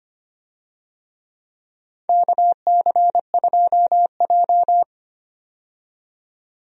Morse code, heard as a keyed tone, about 25 wpm, 700 Hz. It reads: KC2J
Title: USA Call Signs - 4 Characters - 25wpm